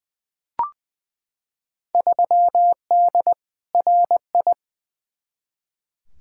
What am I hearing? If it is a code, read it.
Morse code, heard as a keyed tone, about 20 wpm, 700 Hz. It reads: 3D RI
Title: Field Day Exchanges Encore - 20wpm